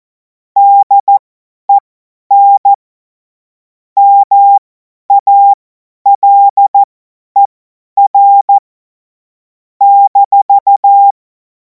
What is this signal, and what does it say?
Morse code, heard as a keyed tone, about 14 wpm, spaced as for 10 wpm, 800 Hz. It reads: DEN MALER -
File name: Morsekurs / Witze / Witz 24. Februar 2026 50 F70